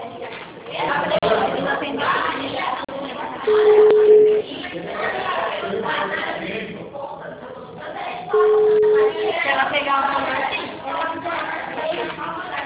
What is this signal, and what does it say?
Ringback tone (Europe, 425 Hz), with speech.